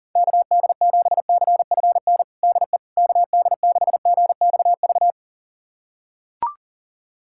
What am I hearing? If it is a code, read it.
Morse code, heard as a keyed tone, about 37 wpm, 700 Hz. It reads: KD7CFN DE KD6GXV
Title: USA Callsign DE Callsign 40wpm